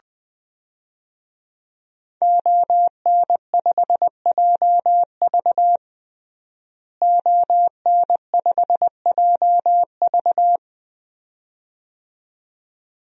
Morse code, heard as a keyed tone, about 20 wpm, 700 Hz. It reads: ON5JV ON5JV
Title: DXCC-100 Call Signs - 5 Characters - Repeated 2x - 20wpm